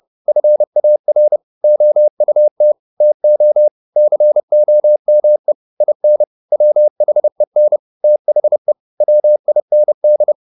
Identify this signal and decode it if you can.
Morse code, heard as a keyed tone, about 30 wpm, 600 Hz. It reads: FAR OUT TO COME IN WHEN THE WIND
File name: the old man and the sea / the old man and the sea 1 30wpm